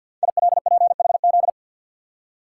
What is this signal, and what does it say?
Morse code, heard as a keyed tone, about 52 wpm, 700 Hz. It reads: ILPHZ
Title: Code Groups - 5 Letter:Numbers - 50wpm